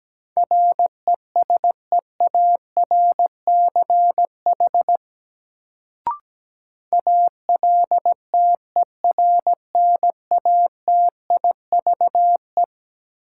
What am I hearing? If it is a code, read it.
Morse code, heard as a keyed tone, about 17 wpm, 700 Hz. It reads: RESEARCH ALTERNATIVE
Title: Sentences from Top 1000 Words - Review Part 4 - New Word Drill - 17wpm